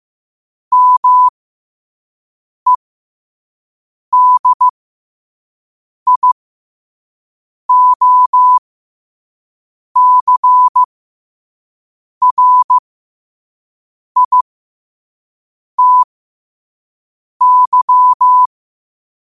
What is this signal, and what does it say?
Morse code, heard as a keyed tone, about 15 wpm, spaced as for 5 wpm, 1000 Hz. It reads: MEDIOCRITY
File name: qotdfiles / 5wpm